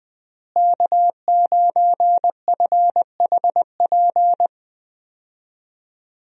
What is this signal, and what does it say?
Morse code, heard as a keyed tone, about 20 wpm, 700 Hz. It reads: K9FHP
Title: USA Call Signs - 5 Characters - Repeated 3x - 20wpm